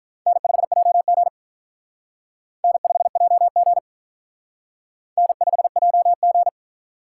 Morse code, heard as a keyed tone, about 43 wpm, 700 Hz. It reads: N5JG N5JG N5JG
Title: Speed Racing - USA 1x2 Call Signs 35wpm